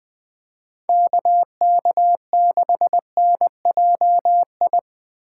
Morse code, heard as a keyed tone, about 19 wpm, 700 Hz. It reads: KK6NJI
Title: Speed Racing - USA 2x3 Call Signs 20wpm